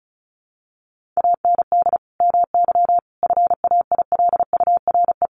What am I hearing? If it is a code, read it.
Morse code, heard as a keyed tone, about 34 wpm, 700 Hz. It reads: AND MY FAILURE
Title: Sentences from Top 1400 Words 35wpm